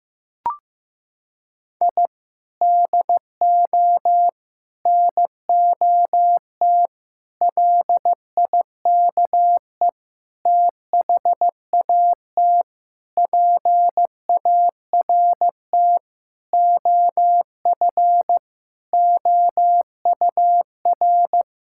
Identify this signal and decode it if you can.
Morse code, heard as a keyed tone, about 14 wpm, 700 Hz. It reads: I DO NOT LIKE THAT PART OF OUR
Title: Sentences from Top 1000 Words - Review Part 6 - 15wpm